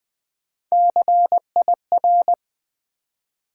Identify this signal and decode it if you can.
Morse code, heard as a keyed tone, about 21 wpm, 700 Hz. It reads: CIR
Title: Build A Word - 3 Letters At A Time - 9-Letter Words - 20wpm